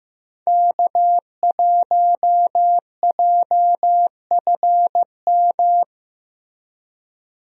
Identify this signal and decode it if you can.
Morse code, heard as a keyed tone, about 15 wpm, 700 Hz. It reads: K1JFM
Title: USA Call Signs - 5 Characters - Repeated 2x - 15wpm